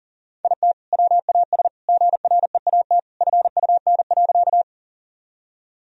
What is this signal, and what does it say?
Morse code, heard as a keyed tone, about 40 wpm, 700 Hz. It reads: IT WAS GREAT FUN.
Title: Sentences from Top 1000 Words - Review Part 4 - 40wpm